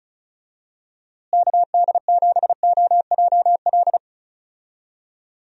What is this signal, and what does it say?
Morse code, heard as a keyed tone, about 36 wpm, 700 Hz. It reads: KD7OJL
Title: USA Call Signs - 6 Characters - Repeated 2x - 35wpm